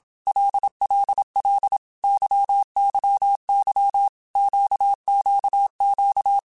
Morse code, heard as a keyed tone, about 27 wpm, 800 Hz. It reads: LLL YYY QQQ
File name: Basic Letter Groups By7Groups / LFCYQ / LFCYQ@25